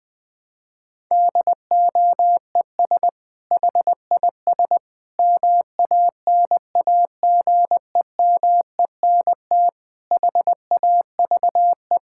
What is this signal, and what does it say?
Morse code, heard as a keyed tone, about 20 wpm, 700 Hz. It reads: DOES HIS MANAGEMENT HAVE
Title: Sentences from Top 1000 Words - Review Part 6 - 20wpm